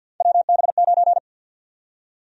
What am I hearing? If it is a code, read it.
Morse code, heard as a keyed tone, about 50 wpm, 700 Hz. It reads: WB9
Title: USA Call Sign Prefixes - 50wpm